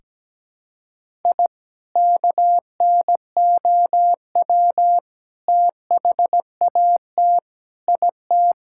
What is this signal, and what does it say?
Morse code, heard as a keyed tone, about 17 wpm, 700 Hz. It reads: I KNOW THAT IT
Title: Sentences from Top 1000 Words - Review Part 6 - 17wpm